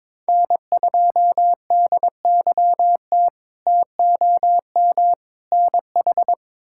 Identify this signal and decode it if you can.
Morse code, heard as a keyed tone, about 22 wpm, 700 Hz. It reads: N2DYT TOM NH
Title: K1USN SST QSO Practice - 22wpm